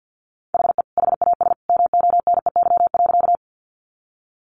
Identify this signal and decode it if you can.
Morse code, heard as a keyed tone, about 50 wpm, 700 Hz. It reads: HE HAS MONEY.
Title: Sentences from Top 400 Words 50wpm